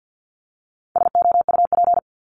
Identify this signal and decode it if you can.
Morse code, heard as a keyed tone, about 48 wpm, 700 Hz. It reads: SOUP
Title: Top 751-1000 Words 50wpm